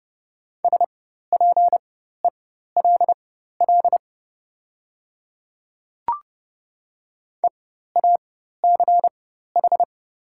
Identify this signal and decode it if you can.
Morse code, heard as a keyed tone, about 30 wpm, spaced as for 14 wpm, 700 Hz. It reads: SPELL EACH
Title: 4x Character Spacing - Top 200 Words - 30wpm